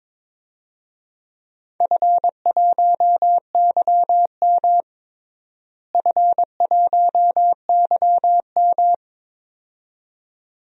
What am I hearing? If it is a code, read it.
Morse code, heard as a keyed tone, about 22 wpm, 700 Hz. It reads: F1YM F1YM
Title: DXCC-100 Call Signs - 4 Characters - Repeated 2x - 22wpm